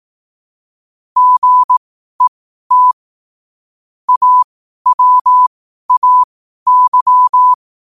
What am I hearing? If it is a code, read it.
Morse code, heard as a keyed tone, about 18 wpm, spaced as for 12 wpm, 1000 Hz. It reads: GET AWAY